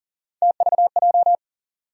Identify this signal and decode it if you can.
Morse code, heard as a keyed tone, about 40 wpm, 700 Hz. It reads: TVJ
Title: USA Call Sign Postfixes - 3 Letters 40wpm